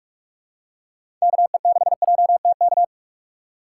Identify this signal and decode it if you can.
Morse code, heard as a keyed tone, about 45 wpm, 700 Hz. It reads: KE6JTK
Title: USA Call Signs - 6x Character Spacing - 45wpm